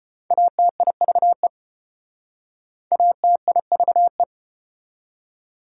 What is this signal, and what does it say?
Morse code, heard as a keyed tone, about 34 wpm, 700 Hz. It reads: ATIVE ATIVE